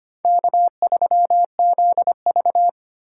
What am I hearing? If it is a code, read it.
Morse code, heard as a keyed tone, about 25 wpm, 700 Hz. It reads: K3ZV
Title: USA Call Signs - 4 Characters - 2x Character Spacing - 25wpm